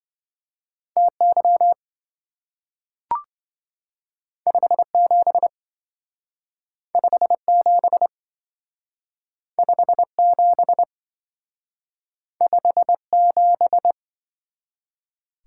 Morse code, heard as a keyed tone, about 27 wpm, 700 Hz. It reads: TY 57 57 57 57